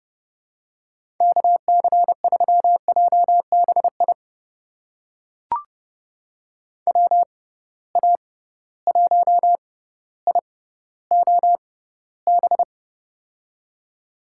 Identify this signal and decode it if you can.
Morse code, heard as a keyed tone, about 32 wpm, spaced as for 16 wpm, 700 Hz. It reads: KC3JBI WA1IOB
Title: USA Call Signs - 6 Characters - 6x Character Spacing - 30wpm